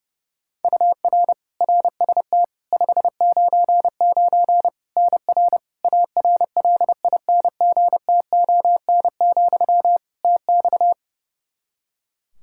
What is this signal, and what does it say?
Morse code, heard as a keyed tone, about 32 wpm, 700 Hz. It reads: UR RST 599 NR ARLINGTON, TX
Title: Practice QSO - Exchange 1 - Beginning - 30wpm